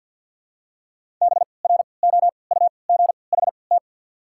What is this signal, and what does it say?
Morse code, heard as a keyed tone, about 54 wpm, spaced as for 29 wpm, 690 Hz. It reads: BROUGHT
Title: 3x Character Spacing - Top 500 Words - 50wpm